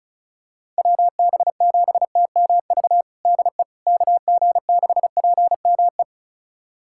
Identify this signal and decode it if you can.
Morse code, heard as a keyed tone, about 34 wpm, 700 Hz. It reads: WB7TMV DE KG6PME
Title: USA Callsign DE Callsign 35wpm